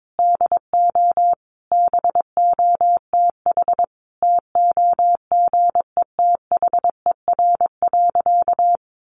Morse code, heard as a keyed tone, about 22 wpm, 700 Hz. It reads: DO BOTH TOGETHER.
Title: Sentences from Top 1300 Words 22wpm